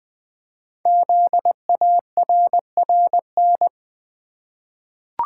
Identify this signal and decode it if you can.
Morse code, heard as a keyed tone, about 20 wpm, 700 Hz. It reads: ZARRN
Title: Code Groups - 5 Letters - 20wpm